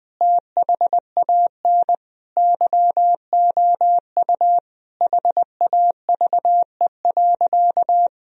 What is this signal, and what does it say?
Morse code, heard as a keyed tone, about 20 wpm, 700 Hz. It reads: THAN YOU HAVE.
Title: Sentences from Top 1000 Words - Review Part 3 - 20wpm